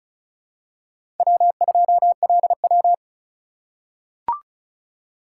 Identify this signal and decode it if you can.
Morse code, heard as a keyed tone, about 36 wpm, 700 Hz. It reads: W2LW
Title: USA Call Signs - 4 Characters - 2x Character Spacing - 35wpm